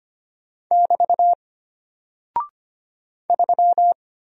Morse code, heard as a keyed tone, about 25 wpm, 700 Hz. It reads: <BT> 3
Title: Single Character (Letter:Number:Punctuation:Prosign) Alt 25wpm